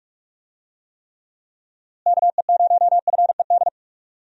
Morse code, heard as a keyed tone, about 44 wpm, 700 Hz. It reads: KE0FED
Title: USA Call Signs - 6 Characters - Repeated 3x - 45wpm